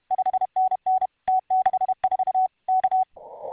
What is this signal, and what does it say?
Morse code, heard as a keyed tone, about 32 wpm, 750 Hz. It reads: 5NN T64 K